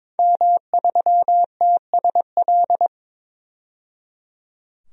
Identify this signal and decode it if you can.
Morse code, heard as a keyed tone, about 21 wpm, 700 Hz. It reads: M3TSL
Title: DXCC-100 Call Signs - 5 Characters - 22wpm